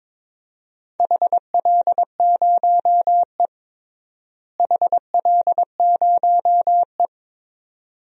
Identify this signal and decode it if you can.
Morse code, heard as a keyed tone, about 21 wpm, 700 Hz. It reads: HL0E HL0E
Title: DXCC-100 Call Signs - 4 Characters - Repeated 2x - 22wpm